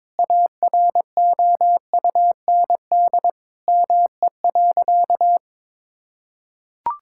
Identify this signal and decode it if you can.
Morse code, heard as a keyed tone, about 22 wpm, 700 Hz. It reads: AROUND ME.
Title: Sentences from Top 1600 Words 22wpm